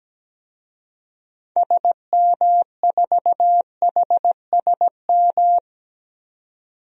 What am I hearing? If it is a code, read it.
Morse code, heard as a keyed tone, about 17 wpm, 700 Hz. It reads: SM4HSM
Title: DXCC-100 Call Signs - 6 Characters - 17wpm